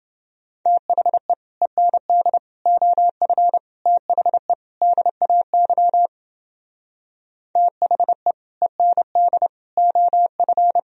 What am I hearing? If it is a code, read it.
Morse code, heard as a keyed tone, about 30 wpm, 700 Hz. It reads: THE END OF THE DAY THE END OF